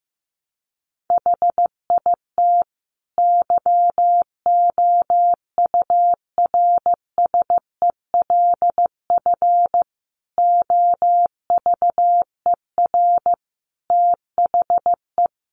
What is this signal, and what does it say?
Morse code, heard as a keyed tone, about 15 wpm, 700 Hz. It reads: HIT YOURSELF OVER THE
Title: Sentences from Top 1200 Words 15wpm